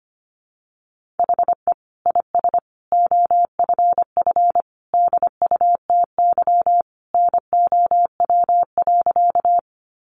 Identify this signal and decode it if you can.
Morse code, heard as a keyed tone, about 25 wpm, 700 Hz. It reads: HE IS OFF DUTY NOW.
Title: Sentences from Top 1500 Words 25wpm